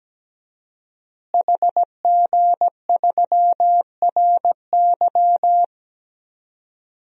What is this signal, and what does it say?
Morse code, heard as a keyed tone, about 17 wpm, 700 Hz. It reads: HG3RY
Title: DXCC-100 Call Signs - 5 Characters - Repeated 3x - 17wpm